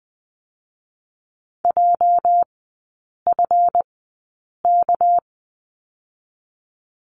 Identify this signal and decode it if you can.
Morse code, heard as a keyed tone, about 20 wpm, spaced as for 8 wpm, 700 Hz. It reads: JFK
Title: 218 Lesson 35e - Number 0 - Three Characters 20wpm